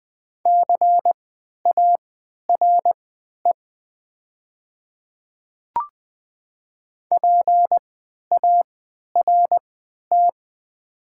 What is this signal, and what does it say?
Morse code, heard as a keyed tone, about 20 wpm, spaced as for 11 wpm, 700 Hz. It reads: CARE PART
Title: 3x Character Spacing - Top 300 Words - 20wpm